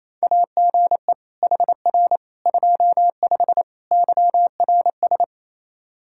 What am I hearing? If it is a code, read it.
Morse code, heard as a keyed tone, about 28 wpm, 700 Hz. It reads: AGE HR 25 YRS
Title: K1USN SST QSO Practice 28wpm